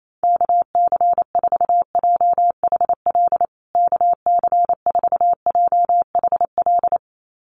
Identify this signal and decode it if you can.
Morse code, heard as a keyed tone, about 28 wpm, 700 Hz. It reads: KC4JHL KC4JHL
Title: USA Call Signs Repeated 2x 28wpm